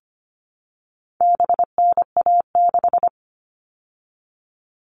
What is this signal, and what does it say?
Morse code, heard as a keyed tone, about 25 wpm, 700 Hz. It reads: BNA6
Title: All Cut Numbers - 4-Digit 25wpm